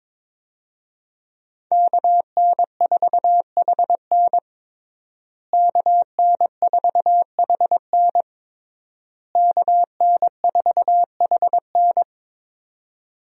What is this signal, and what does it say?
Morse code, heard as a keyed tone, about 22 wpm, 700 Hz. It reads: KN4HN KN4HN KN4HN
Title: USA Call Signs - 5 Characters - Repeated 3x - 22wpm